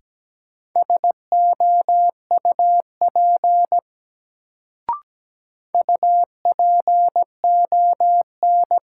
Morse code, heard as a keyed tone, about 17 wpm, 700 Hz. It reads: SOUP UPON
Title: Sentences from Top 1000 Words - Review Part 6 - New Word Drill - 17wpm